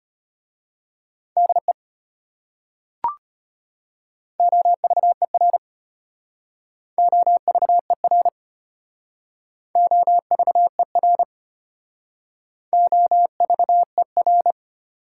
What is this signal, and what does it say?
Morse code, heard as a keyed tone, about 32 wpm, 700 Hz. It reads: DE OVER OVER OVER OVER